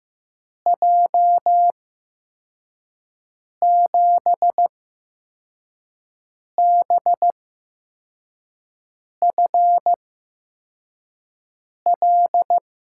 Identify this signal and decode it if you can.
Morse code, heard as a keyed tone, about 15 wpm, spaced as for 4 wpm, 700 Hz. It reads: J7BFL
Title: USA Call Signs - 6 Characters - 8x Character Spacing - 15wpm